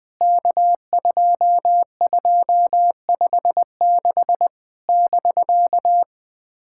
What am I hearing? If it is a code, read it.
Morse code, heard as a keyed tone, about 19 wpm, 700 Hz. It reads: K2256 <BK>
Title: POTA QSO Practice - 20wpm